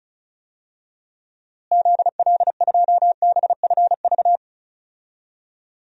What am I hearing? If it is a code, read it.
Morse code, heard as a keyed tone, about 36 wpm, 700 Hz. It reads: ZL2BFV
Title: DXCC-100 Call Signs - 6 Characters - Repeated 3x - 35wpm